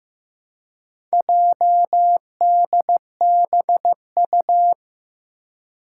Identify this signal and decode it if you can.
Morse code, heard as a keyed tone, about 15 wpm, 700 Hz. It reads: JDBU
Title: Four Challenging Letter-Number Combinations 15wpm